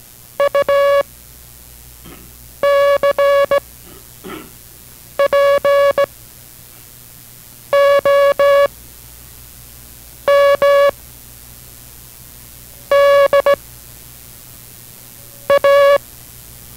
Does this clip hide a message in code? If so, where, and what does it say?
Yes: Morse code from 0.4 to 16.0 s, heard as a keyed tone, about 17 wpm, 560 Hz. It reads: U C P O M D A